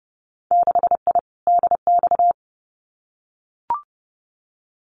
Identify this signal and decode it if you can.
Morse code, heard as a keyed tone, about 30 wpm, 700 Hz. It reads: 6I DX
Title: Winter Field Day Exchanges Encore 30wpm